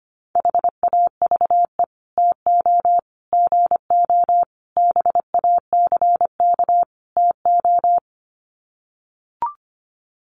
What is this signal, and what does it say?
Morse code, heard as a keyed tone, about 25 wpm, 700 Hz. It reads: HAVE TO GO BACK TO
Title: Sets of 5 Words Encore2 25wpm